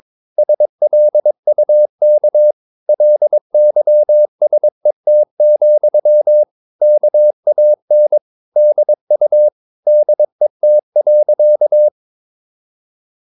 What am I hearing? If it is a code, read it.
Morse code, heard as a keyed tone, about 22 wpm, 600 Hz. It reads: SLUK LYSET, KAN DU DET.
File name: Ferie QRS 22wpm